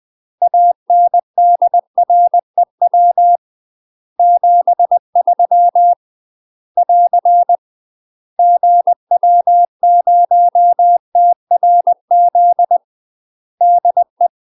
Morse code, heard as a keyed tone, about 20 wpm, 700 Hz. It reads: ANDREW 73 <AR> GW0TRZ DE
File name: UK MASTER QSOs TRAINING / 20WPM / UK MASTER QSOs 0044